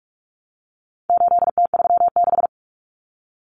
Morse code, heard as a keyed tone, about 44 wpm, 700 Hz. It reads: 8T36
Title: Common Cut Numbers - 4-Digit 45wpm